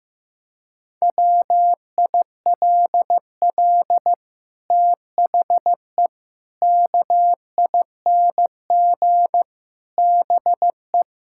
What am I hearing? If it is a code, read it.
Morse code, heard as a keyed tone, about 14 wpm, 700 Hz. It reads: WILL THE KING BE
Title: Sentences from Top 1000 Words - Review Part 6 - 15wpm